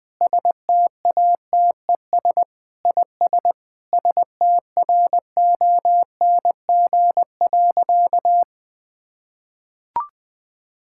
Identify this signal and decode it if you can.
Morse code, heard as a keyed tone, about 20 wpm, 700 Hz. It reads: STATES IS STRONG.
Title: Sentences with 8-Word Phrases - Spoken Beginning - 20wpm